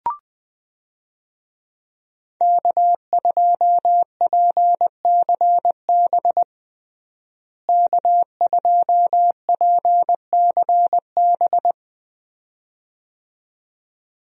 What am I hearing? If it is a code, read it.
Morse code, heard as a keyed tone, about 20 wpm, 700 Hz. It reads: K2PCB K2PCB
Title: USA Call Signs - 5 Characters - Repeated 2x - 20wpm